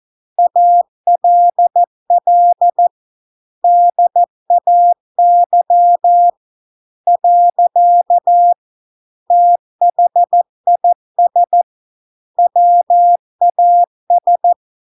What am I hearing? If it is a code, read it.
Morse code, heard as a keyed tone, about 14 wpm, 700 Hz. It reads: ALL DAY . THIS WAS